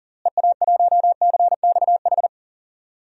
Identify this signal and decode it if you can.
Morse code, heard as a keyed tone, about 40 wpm, 700 Hz. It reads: EA1CXH
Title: DXCC-100 Call Signs - 6 Characters - Repeated 3x - 40wpm